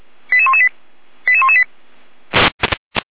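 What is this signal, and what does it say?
Five-tone selective call (ZVEI), heard as quick runs of five tones, tones of about 70 ms. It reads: 88118 88118
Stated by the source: Received on SDR as ZVEI five-tone selective call 88207, from audio.f85360.feuerwehr-orlamuende.org.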